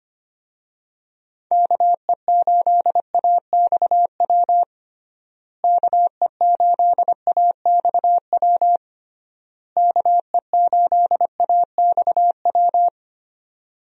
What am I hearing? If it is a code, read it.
Morse code, heard as a keyed tone, about 25 wpm, 700 Hz. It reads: KE8AXW KE8AXW KE8AXW
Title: USA Call Signs - 6 Characters - Repeated 3x - 25wpm